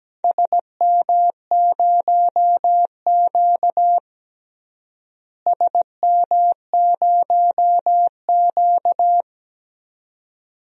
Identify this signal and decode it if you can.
Morse code, heard as a keyed tone, about 17 wpm, 700 Hz. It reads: SM0Q SM0Q
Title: DXCC-100 Call Signs - 4 Characters - Repeated 2x - 17wpm